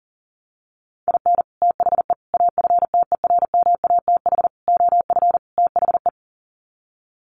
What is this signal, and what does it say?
Morse code, heard as a keyed tone, about 40 wpm, 700 Hz. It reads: IN THE AFTERMATH OF THE
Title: Sets of 5 Words 40wpm